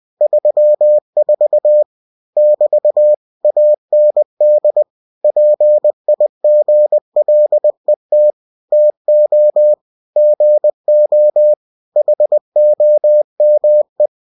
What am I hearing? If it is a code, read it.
Morse code, heard as a keyed tone, about 20 wpm, 600 Hz. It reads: 34 <BT> AND PIGLET TO GO HOME
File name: cw-books / winnie-the-pooh / winnie-20wpm-page-0034